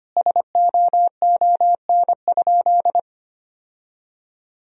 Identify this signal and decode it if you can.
Morse code, heard as a keyed tone, about 25 wpm, 700 Hz. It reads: SOON?
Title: Sentences from Top 1000 Words - Review Part 1 - 25wpm